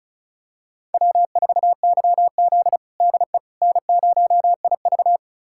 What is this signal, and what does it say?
Morse code, heard as a keyed tone, about 36 wpm, 700 Hz. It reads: W4YZ DE N0IV
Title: USA Callsign DE Callsign - 4 Characters - 35wpm